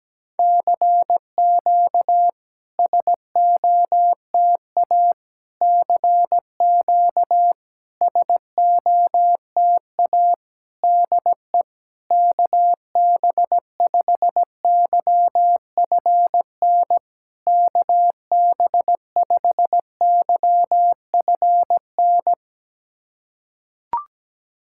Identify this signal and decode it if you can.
Morse code, heard as a keyed tone, about 17 wpm, 700 Hz. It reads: CQ SOTA CQ SOTA DE KB5YFN KB5YFN
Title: SOTA QSO Practice 17wpm